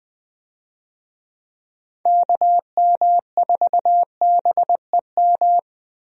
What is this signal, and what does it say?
Morse code, heard as a keyed tone, about 20 wpm, 700 Hz. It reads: KM4BEM
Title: USA Call Signs - 6 Characters - Repeated 2x - 20wpm